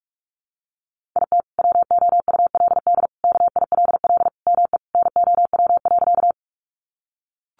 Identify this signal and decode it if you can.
Morse code, heard as a keyed tone, about 44 wpm, 700 Hz. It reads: IT WOULD KILL ME NOW.
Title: Sentences from Top 1400 Words 45wpm